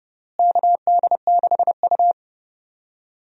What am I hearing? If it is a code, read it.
Morse code, heard as a keyed tone, about 30 wpm, 700 Hz. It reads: KD6U
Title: USA Call Signs - 4 Characters - Repeated 3x - 30wpm